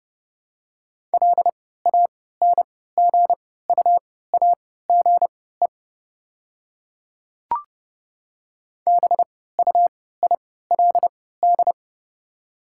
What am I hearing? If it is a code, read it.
Morse code, heard as a keyed tone, about 30 wpm, spaced as for 17 wpm, 700 Hz. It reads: LANGUAGE BUILD
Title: 3x Character Spacing - Top 500 Words - 30wpm